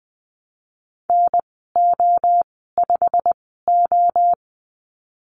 Morse code, heard as a keyed tone, about 20 wpm, spaced as for 15 wpm, 700 Hz. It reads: NO5O
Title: 161 Lesson 26g - Letter g - Call Signs 20wpm